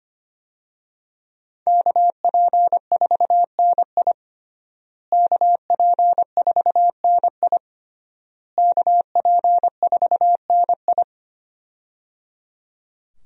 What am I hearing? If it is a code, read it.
Morse code, heard as a keyed tone, about 25 wpm, 700 Hz. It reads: KP4NI KP4NI KP4NI
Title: USA Call Signs - 5 Characters - Repeated 3x - 25wpm